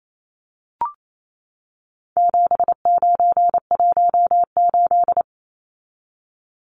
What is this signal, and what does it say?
Morse code, heard as a keyed tone, about 28 wpm, 700 Hz. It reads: 7918
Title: Numbers - 4-Digit 28wpm